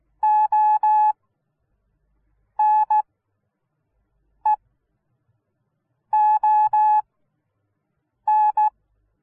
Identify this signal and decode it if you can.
Morse code, heard as a keyed tone, about 16 wpm, spaced as for 5 wpm, 850 Hz. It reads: ONEON